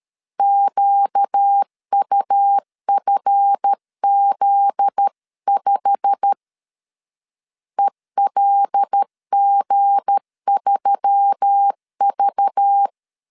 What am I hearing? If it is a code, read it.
Morse code, heard as a keyed tone, about 13 wpm, 800 Hz. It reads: QUFZ5 ELG3V
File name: LEZIONI / 10 SIGNS / 10D COMA 12wpm 2544